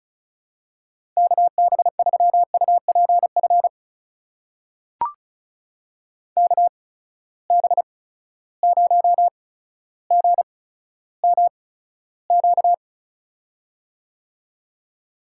Morse code, heard as a keyed tone, about 36 wpm, spaced as for 15 wpm, 700 Hz. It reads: KB3UPF KB0GMQ
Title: USA Call Signs - 6 Characters - 8x Character Spacing - 35wpm